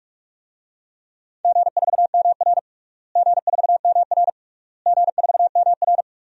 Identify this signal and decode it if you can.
Morse code, heard as a keyed tone, about 46 wpm, 700 Hz. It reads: G4MR G4MR G4MR
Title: DXCC-100 Call Signs - 4 Characters - Repeated 3x - 45wpm